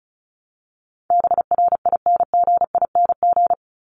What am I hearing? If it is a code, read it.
Morse code, heard as a keyed tone, about 34 wpm, 700 Hz. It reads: BRINGING
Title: ING Suffix - Top 100 Words with Base Words 35wpm